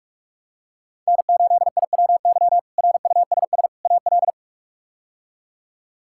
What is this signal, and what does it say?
Morse code, heard as a keyed tone, about 46 wpm, 700 Hz. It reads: N8IWY RUSS AL
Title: K1USN SST QSO Practice - 45wpm